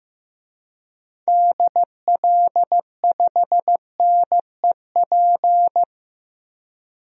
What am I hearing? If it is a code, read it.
Morse code, heard as a keyed tone, about 15 wpm, 700 Hz. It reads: DL5NEP
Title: DXCC-100 Call Signs - 6 Characters - 15wpm